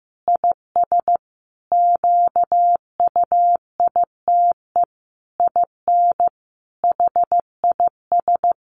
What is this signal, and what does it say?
Morse code, heard as a keyed tone, about 15 wpm, 700 Hz. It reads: IS QUITE IN HIS
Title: Sentences from Top 1400 Words 15wpm